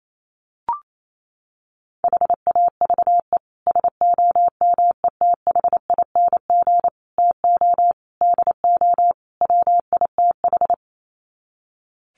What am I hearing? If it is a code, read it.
Morse code, heard as a keyed tone, about 28 wpm, 700 Hz. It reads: HAVE SOMETHING TO DO WITH
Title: Sets of 5 Words Encore 28wpm